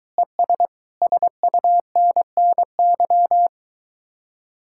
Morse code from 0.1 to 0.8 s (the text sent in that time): ES